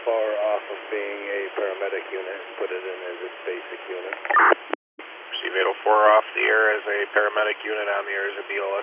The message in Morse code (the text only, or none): none